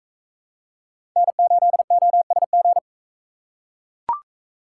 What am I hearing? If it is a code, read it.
Morse code, heard as a keyed tone, about 40 wpm, 700 Hz. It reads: N8OSG